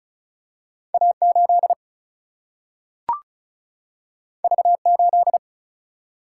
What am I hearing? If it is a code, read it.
Morse code, heard as a keyed tone, about 36 wpm, 700 Hz. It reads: A 8 V 8
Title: Single-Letter Single-Number 35wpm